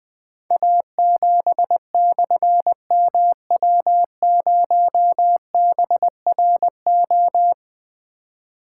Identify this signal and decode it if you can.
Morse code, heard as a keyed tone, about 20 wpm, 700 Hz. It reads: A7/MW0BRO
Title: Difficult Call Signs 20wpm